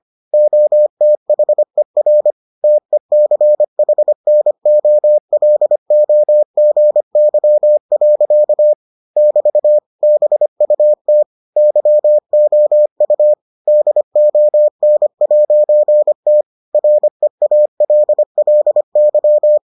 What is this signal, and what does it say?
Morse code, heard as a keyed tone, about 25 wpm, 600 Hz. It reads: OTHER TECHNOLOGY. <BT> BUT YOU DON'T REALLY